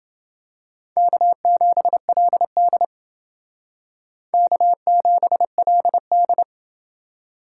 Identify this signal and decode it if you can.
Morse code, heard as a keyed tone, about 28 wpm, 700 Hz. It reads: K7LD K7LD